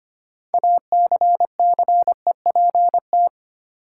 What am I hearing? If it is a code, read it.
Morse code, heard as a keyed tone, about 25 wpm, 700 Hz. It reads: ACCEPT
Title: Sentences from Top 1300 Words - New Word Drill - 25wpm